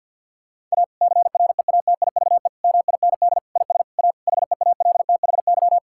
Morse code, heard as a keyed tone, about 50 wpm, 700 Hz. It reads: A CREATIVE MIND IS A HEALTHY
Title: Sentences from Top 1000 Words - Review Part 5 - 50wpm